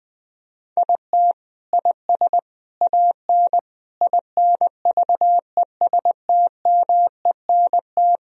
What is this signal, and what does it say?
Morse code, heard as a keyed tone, about 20 wpm, 700 Hz. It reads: IT IS AN INVESTMENT
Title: Sentences from Top 1000 Words - Review Part 4 - 20wpm